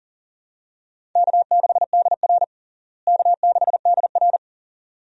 Morse code, heard as a keyed tone, about 37 wpm, 700 Hz. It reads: K6DR K6DR
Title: USA Call Signs - 4 Characters - Repeated 2x - 40wpm